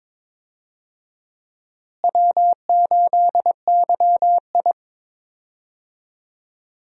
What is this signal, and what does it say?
Morse code, heard as a keyed tone, about 22 wpm, 700 Hz. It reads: W8YI
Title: USA Call Signs - 4 Characters - Repeated 3x - 22wpm